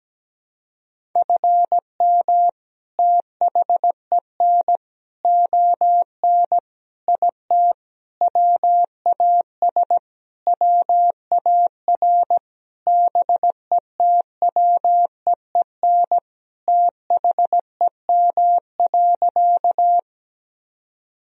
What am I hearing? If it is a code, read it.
Morse code, heard as a keyed tone, about 17 wpm, 700 Hz. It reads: FM THEN ON IT WAS WAR BETWEEN THEM.
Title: Sentences from Top 400 Words and Abbreviations 17wpm